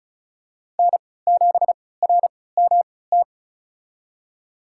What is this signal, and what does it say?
Morse code, heard as a keyed tone, about 36 wpm, spaced as for 20 wpm, 700 Hz. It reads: N7RMT
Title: USA Call Signs - 5 Characters - 3x Character Spacing - 35wpm